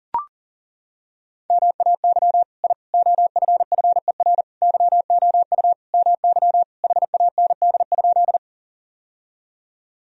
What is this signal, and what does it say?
Morse code, heard as a keyed tone, about 40 wpm, 700 Hz. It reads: MAY I OFFER YOU MY HAND?
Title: Sentences from Top 1000 Words - Review Part 4 - 40wpm